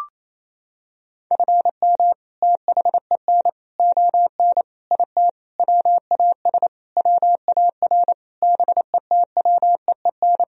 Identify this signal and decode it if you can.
Morse code, heard as a keyed tone, about 27 wpm, 700 Hz. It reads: FM THEN ON IT WAS WAR BETWEEN
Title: Sentences from Top 400 Words and Abbreviations 28wpm